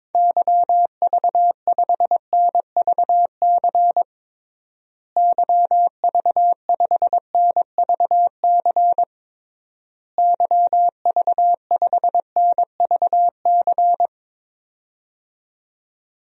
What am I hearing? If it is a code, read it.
Morse code, heard as a keyed tone, about 22 wpm, 700 Hz. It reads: YV5NVC YV5NVC YV5NVC
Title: DXCC-100 Call Signs - 6 Characters - Repeated 3x - 22wpm